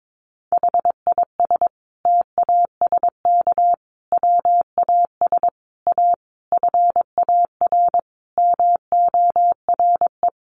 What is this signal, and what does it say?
Morse code, heard as a keyed tone, about 22 wpm, 700 Hz. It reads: HIS TASK WAS A FAR MORE
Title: Sentences from Top 1200 Words 22wpm